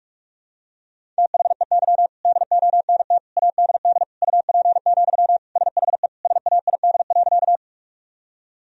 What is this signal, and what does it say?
Morse code, heard as a keyed tone, about 46 wpm, 700 Hz. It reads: THEY DONT ADD UP, SHE SAID.
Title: Sentences from Top 1000 Words - Review Part 1 - 45wpm